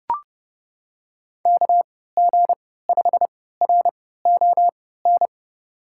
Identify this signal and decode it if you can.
Morse code, heard as a keyed tone, about 31 wpm, spaced as for 17 wpm, 700 Hz. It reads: KG5RON
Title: USA Call Signs - 6 Characters - 3x Character Spacing - 30wpm